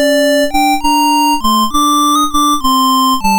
There is no code.